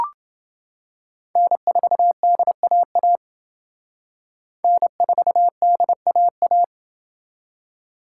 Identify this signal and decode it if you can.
Morse code, heard as a keyed tone, about 29 wpm, 700 Hz. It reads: N4DAA N4DAA